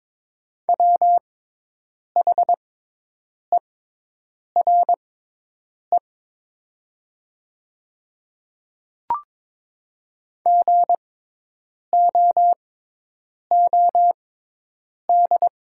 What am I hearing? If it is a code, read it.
Morse code, heard as a keyed tone, about 22 wpm, spaced as for 8 wpm, 700 Hz. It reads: WHERE GOOD
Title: 6x Character Spacing - Top 200 Words - 22wpm